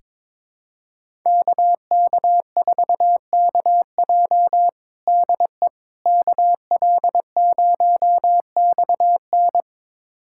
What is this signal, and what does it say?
Morse code, heard as a keyed tone, about 22 wpm, 700 Hz. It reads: KK4KJ DE KL0XN
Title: USA Callsign DE Callsign - 5 Characters - 22wpm